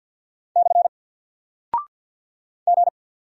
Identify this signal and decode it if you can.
Morse code, heard as a keyed tone, about 48 wpm, 700 Hz. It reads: / G
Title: Single Character (Letter:Number:Punctuation:Prosign) 50wpm